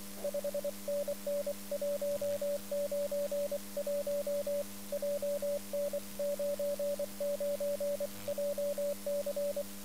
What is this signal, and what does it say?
Morse code, heard as a keyed tone, about 24 wpm, 580 Hz, with radio noise. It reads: 5NN 191 JN99JC
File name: iw0ffk / 2013 11 02 OM6A IW0FFK OM side